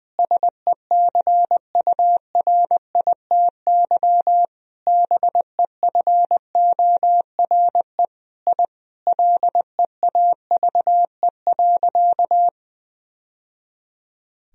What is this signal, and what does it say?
Morse code, heard as a keyed tone, about 21 wpm, 700 Hz. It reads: SECURITY BEFORE I LEAVE.
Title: Sentences from Top 1000 Words - Review Part 4 - 20wpm